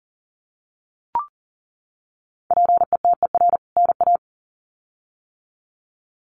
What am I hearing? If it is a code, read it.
Morse code, heard as a keyed tone, about 40 wpm, 700 Hz. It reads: PETER NA
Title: CWT Exchange 40wpm